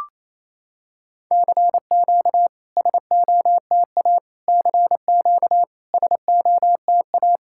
Morse code, heard as a keyed tone, about 28 wpm, 700 Hz. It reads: CQ SOTA CQ SOTA
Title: SOTA QSO Practice 28wpm